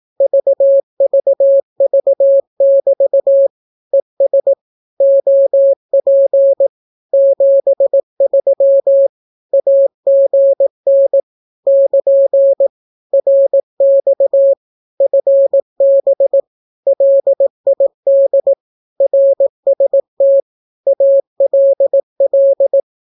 Morse code, heard as a keyed tone, about 18 wpm, 550 Hz. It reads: VVV<BT> ES OP 73 AGN <KN> RX FB LID RST ALL
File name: naukapodstaw / skroty / skroty